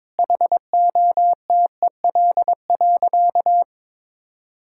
Morse code, heard as a keyed tone, about 22 wpm, 700 Hz. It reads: HOTEL.
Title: Sentences from Top 1600 Words 22wpm